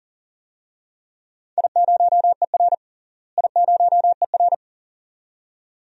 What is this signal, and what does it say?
Morse code, heard as a keyed tone, about 40 wpm, 700 Hz. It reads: I0ER I0ER
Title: DXCC-100 Call Signs - 4 Characters - Repeated 2x - 40wpm